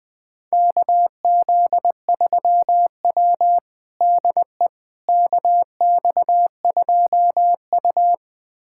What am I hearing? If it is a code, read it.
Morse code, heard as a keyed tone, about 20 wpm, 700 Hz. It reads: KZ3W DE KX2U
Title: USA Callsign DE Callsign - 4 Characters - 20wpm